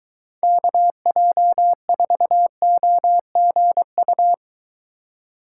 Morse code, heard as a keyed tone, about 22 wpm, 700 Hz. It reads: KJ4OGU